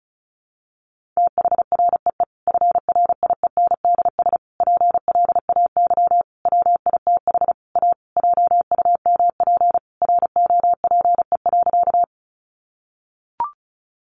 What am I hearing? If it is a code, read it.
Morse code, heard as a keyed tone, about 34 wpm, 700 Hz. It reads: THREE FRIENDS PLAY WITH A JUMP ROPE.
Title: Easy Phrases 35wpm